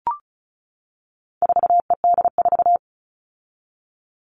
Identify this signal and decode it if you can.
Morse code, heard as a keyed tone, about 34 wpm, 700 Hz. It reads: 4ED4
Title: All Cut Numbers - 4-Digit 35wpm